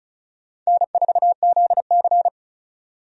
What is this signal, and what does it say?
Morse code, heard as a keyed tone, about 34 wpm, 700 Hz. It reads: N4ZC
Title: USA Call Signs - 4 Characters - 35wpm